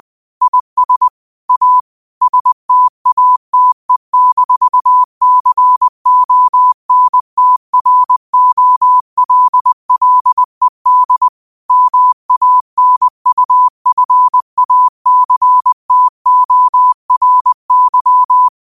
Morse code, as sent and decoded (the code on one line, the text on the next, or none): .. ... / .- / ... - .- - . -....- -.-. --- -. - .-. --- .-.. .-.. -.. / -- .- -. ..- ..-. .- -.-. - --- .-. -.--
IS A STATE-CONTROLLED MANUFACTORY